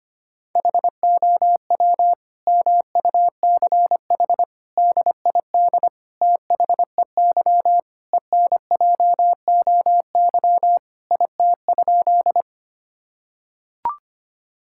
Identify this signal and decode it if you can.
Morse code, heard as a keyed tone, about 25 wpm, 700 Hz. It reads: HOW MUCH DID THEY ENJOY IT?
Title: Sentences from Top 1000 Words - Review Part 5 - 25wpm